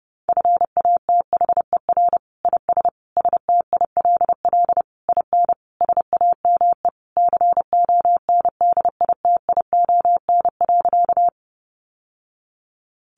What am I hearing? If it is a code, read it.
Morse code, heard as a keyed tone, about 30 wpm, 700 Hz. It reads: FATHER IS STILL IN SAME CONDITION.
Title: Sentences from Top 1000 Words 30wpm